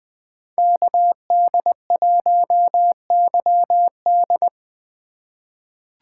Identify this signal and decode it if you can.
Morse code, heard as a keyed tone, about 21 wpm, 700 Hz. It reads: KD1YD
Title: USA Call Signs - 2x Character Spacing - 20wpm